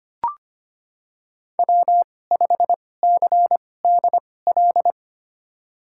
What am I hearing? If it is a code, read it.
Morse code, heard as a keyed tone, about 25 wpm, spaced as for 18 wpm, 700 Hz. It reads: W5CDL
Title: USA Call Signs - 5 Characters - 2x Character Spacing - 25wpm